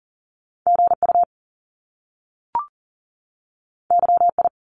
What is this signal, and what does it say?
Morse code, heard as a keyed tone, about 42 wpm, 700 Hz. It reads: GU YI
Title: Top Two Letter Combinations in Words 40wpm